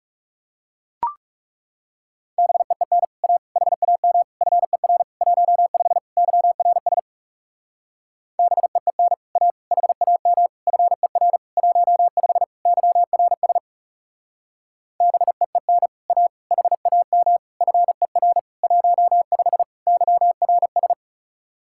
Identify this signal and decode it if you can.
Morse code, heard as a keyed tone, about 37 wpm, 700 Hz. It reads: BEEN A HAM FER 15 YRS BEEN A HAM FER 15 YRS BEEN A HAM FER 15 YRS
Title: Speed-Racing Common QSO Phrases - 30wpm